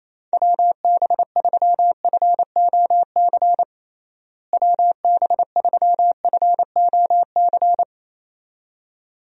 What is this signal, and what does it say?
Morse code, heard as a keyed tone, about 28 wpm, 700 Hz. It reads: WB3FOC WB3FOC
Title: USA Call Signs - 6 Characters - Repeated 2x - 28wpm